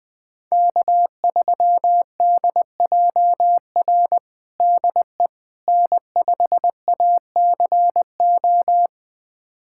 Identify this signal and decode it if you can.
Morse code, heard as a keyed tone, about 21 wpm, 700 Hz. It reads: K3DJR DE N5ACO
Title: USA Callsign DE Callsign - 5 Characters - 20wpm